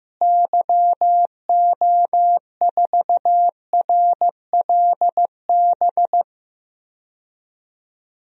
Morse code, heard as a keyed tone, about 15 wpm, 700 Hz. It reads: YO4RLB
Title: DXCC-100 Call Signs - 6 Characters - Repeated 2x - 15wpm